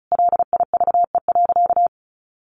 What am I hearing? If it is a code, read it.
Morse code, heard as a keyed tone, about 34 wpm, 700 Hz. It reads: LIVE.
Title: Sentences from Top 800 Words 35wpm